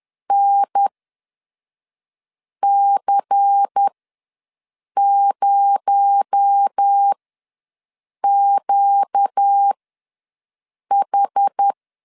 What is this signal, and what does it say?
Morse code, heard as a keyed tone, about 10 wpm, spaced as for 5 wpm, 800 Hz. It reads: NC0QH
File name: LEZIONI / 6 PKX0 / 6 PKX0 ws 10wpm 4220